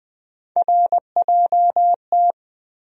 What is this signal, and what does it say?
Morse code, heard as a keyed tone, about 20 wpm, 700 Hz. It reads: RJT
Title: USA Call Sign Postfixes - 3 Letters 20wpm